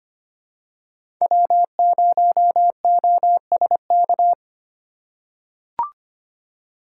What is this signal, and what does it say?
Morse code, heard as a keyed tone, about 25 wpm, 700 Hz. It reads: W0OSK
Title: USA Call Signs - 5 Characters - 3x Character Spacing - 25wpm